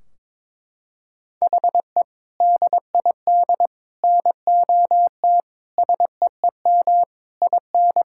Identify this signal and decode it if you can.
Morse code, heard as a keyed tone, about 22 wpm, 700 Hz. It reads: HE DID NOT SEEM IN
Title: Sentences from Top 1600 Words 22wpm